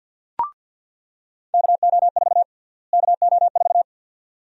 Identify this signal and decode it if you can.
Morse code, heard as a keyed tone, about 47 wpm, 700 Hz. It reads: KO4 KO4
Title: USA Call Sign Prefixes - Repeated 2x - 50wpm